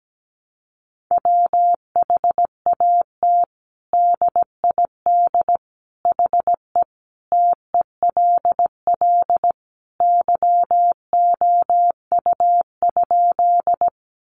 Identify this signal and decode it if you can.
Morse code, heard as a keyed tone, about 17 wpm, 700 Hz. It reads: WHAT DID HE TELL YOU?
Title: Sentences from Top 200 Words 17wpm